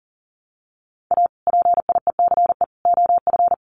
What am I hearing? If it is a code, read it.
Morse code, heard as a keyed tone, about 40 wpm, 700 Hz. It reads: A PIECE OF
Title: Sets of 3 Words 40wpm